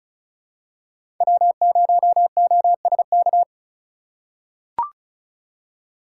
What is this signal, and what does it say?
Morse code, heard as a keyed tone, about 34 wpm, 700 Hz. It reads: W0OSK
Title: USA Call Signs - 5 Characters - 3x Character Spacing - 35wpm